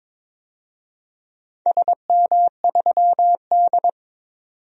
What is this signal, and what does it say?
Morse code, heard as a keyed tone, about 22 wpm, 700 Hz. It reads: SM3D
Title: DXCC-100 Call Signs - 4 Characters - 22wpm